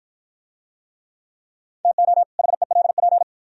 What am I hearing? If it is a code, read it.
Morse code, heard as a keyed tone, about 49 wpm, 700 Hz. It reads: TO HELP